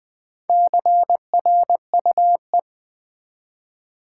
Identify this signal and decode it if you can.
Morse code, heard as a keyed tone, about 20 wpm, 700 Hz. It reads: CRUE
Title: Build A Word - 1 Letter At A Time - 5-Letter Words - 20wpm